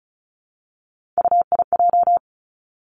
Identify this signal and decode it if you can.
Morse code, heard as a keyed tone, about 34 wpm, 700 Hz. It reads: UIJ
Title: Three Challenging Letter-Number Combinations 35wpm